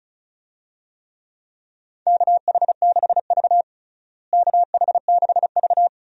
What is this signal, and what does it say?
Morse code, heard as a keyed tone, about 36 wpm, 700 Hz. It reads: KH6V KH6V
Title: DXCC-100 Call Signs - 4 Characters - Repeated 2x - 35wpm